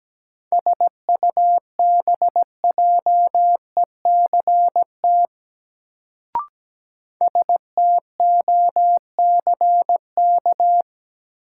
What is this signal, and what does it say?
Morse code, heard as a keyed tone, about 17 wpm, 700 Hz. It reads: SUBJECT STOCK
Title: Sentences from Top 800 Words - New Word Drill - 17wpm